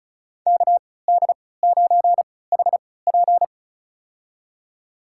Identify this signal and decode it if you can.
Morse code, heard as a keyed tone, about 36 wpm, spaced as for 20 wpm, 700 Hz. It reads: KD9HP
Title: USA Call Signs - 5 Characters - 3x Character Spacing - 35wpm